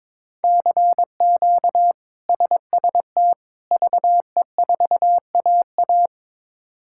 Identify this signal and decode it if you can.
Morse code, heard as a keyed tone, about 22 wpm, 700 Hz. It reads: CQ SST VE4AA
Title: K1USN SST QSO Practice - 22wpm